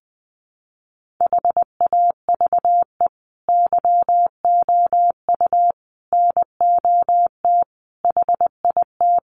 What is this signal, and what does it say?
Morse code, heard as a keyed tone, about 20 wpm, 700 Hz. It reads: HAVE YOU NOT HIT
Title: Sentences from Top 1200 Words 20wpm